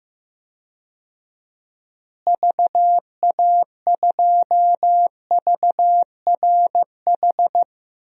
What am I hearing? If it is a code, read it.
Morse code, heard as a keyed tone, about 15 wpm, 700 Hz. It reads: VA2VRH
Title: DXCC-100 Call Signs - 6 Characters - Repeated 3x - 15wpm